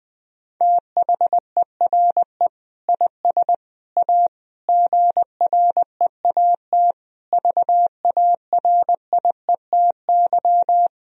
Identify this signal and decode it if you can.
Morse code, heard as a keyed tone, about 20 wpm, 700 Hz. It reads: THERE IS A GREAT VARIETY
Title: Sentences from Top 1000 Words - Review Part 4 - 20wpm